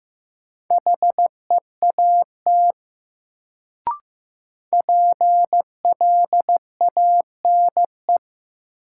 Morse code, heard as a keyed tone, about 15 wpm, 700 Hz. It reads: HEAT PLANE
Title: Sentences from Top 1000 Words - Review Part 3 - New Word Drill - 15wpm